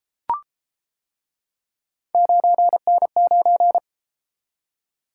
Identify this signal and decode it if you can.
Morse code, heard as a keyed tone, about 31 wpm, 700 Hz. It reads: 9N9